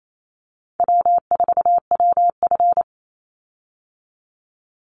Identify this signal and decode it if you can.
Morse code, heard as a keyed tone, about 28 wpm, 700 Hz. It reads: W4WF
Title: Easy Call Signs Repeated 2x - 28wpm